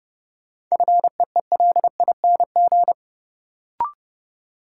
Morse code, heard as a keyed tone, about 29 wpm, 700 Hz. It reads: FEELING
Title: Sentences from Top 1000 Words - Review Part 3 - New Word Drill - 30wpm